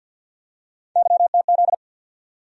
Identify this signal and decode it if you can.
Morse code, heard as a keyed tone, about 49 wpm, 700 Hz. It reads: YTZ
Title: USA Call Sign Postfixes - 3 Letters 50wpm